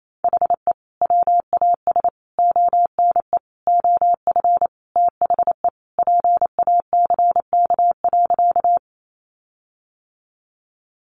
Morse code, heard as a keyed tone, about 28 wpm, 700 Hz. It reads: HE WAS ONE OF THE PACK.
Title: Sentences from Top 1300 Words 28wpm